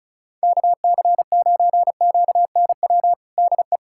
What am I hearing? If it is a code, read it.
Morse code, heard as a keyed tone, about 36 wpm, 700 Hz. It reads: KC9QNW DE
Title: USA Callsign DE Callsign 35wpm